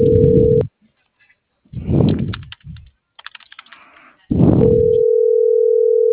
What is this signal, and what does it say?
Ringback tone (North America), 440 + 480 Hz.
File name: CENTRAL MULTIMODAL / out viridiana.juarez